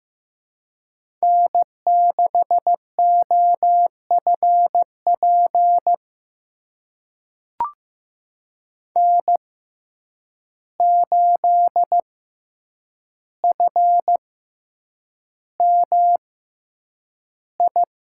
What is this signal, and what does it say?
Morse code, heard as a keyed tone, about 15 wpm, spaced as for 8 wpm, 700 Hz. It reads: N6OFP N8FMI
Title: USA Call Signs - 5 Characters - 6x Character Spacing - 15wpm